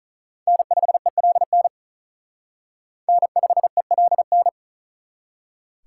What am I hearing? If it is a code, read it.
Morse code, heard as a keyed tone, about 46 wpm, 700 Hz. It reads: N5ELN N5ELN